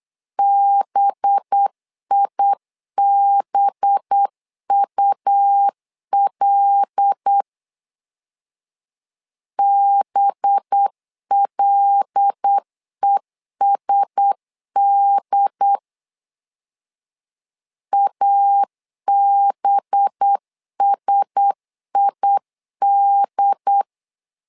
Morse code, heard as a keyed tone, about 8 wpm, 800 Hz. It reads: BIBUL BLESD ABSID
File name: LEZIONI / 5 LBVC / 5 LB 8wpm 2816